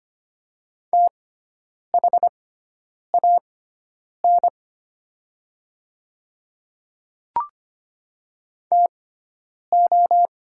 Morse code, heard as a keyed tone, about 25 wpm, spaced as for 9 wpm, 700 Hz. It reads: THAN TO
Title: 6x Character Spacing - Top 100 Words - 25wpm